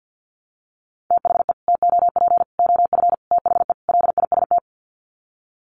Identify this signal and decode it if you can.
Morse code, heard as a keyed tone, about 50 wpm, 700 Hz. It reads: THE TOP OF THE LIST
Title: Sets of 5 Words Encore2 50wpm